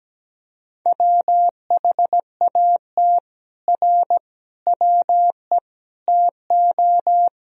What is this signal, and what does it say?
Morse code, heard as a keyed tone, about 17 wpm, 700 Hz. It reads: WHAT R WE TO
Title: Sentences from Top 100 Words and Abbreviations 17wpm